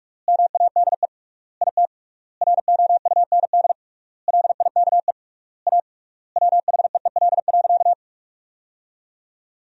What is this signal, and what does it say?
Morse code, heard as a keyed tone, about 46 wpm, 700 Hz. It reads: MADE IT ROUND LIKE A WHEEL.
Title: 3x Word-Spacing - Sentences from Top 500 Words - 45wpm